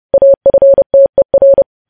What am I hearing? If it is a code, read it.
Morse code, heard as a keyed tone, about 30 wpm, 550 Hz. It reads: AFTER
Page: news headlines in morse code at 30 wpm